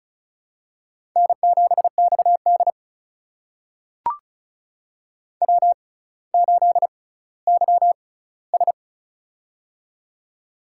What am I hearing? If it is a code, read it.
Morse code, heard as a keyed tone, about 36 wpm, spaced as for 18 wpm, 700 Hz. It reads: N7XD W8YS
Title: USA Call Signs - 4 Characters - 6x Character Spacing - 35wpm